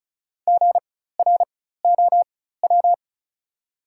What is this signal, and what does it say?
Morse code, heard as a keyed tone, about 36 wpm, spaced as for 16 wpm, 690 Hz. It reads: GROW
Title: 4x Character Spacing - Top 300 Words - 35wpm